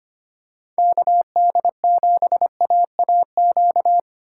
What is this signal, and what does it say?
Morse code, heard as a keyed tone, about 25 wpm, 700 Hz. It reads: KD7AAQ
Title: USA Call Signs - 6 Characters - 2x Character Spacing - 25wpm